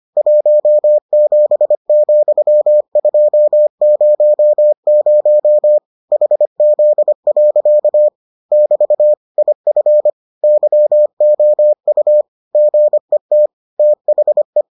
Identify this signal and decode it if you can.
Morse code, heard as a keyed tone, about 25 wpm, 600 Hz. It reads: 17,200 HZ. <BT> IF YOU GET THE